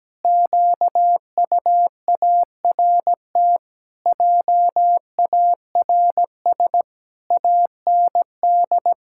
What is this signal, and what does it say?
Morse code, heard as a keyed tone, about 17 wpm, 700 Hz. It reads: QUART JARS AND
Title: Pangram Challenge v2 - 17wpm